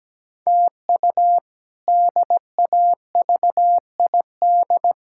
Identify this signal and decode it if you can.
Morse code, heard as a keyed tone, about 17 wpm, 700 Hz. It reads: TU DAVID
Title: K1USN SST QSO Practice - 17wpm